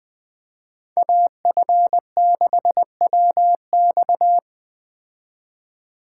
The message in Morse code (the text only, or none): AF6WX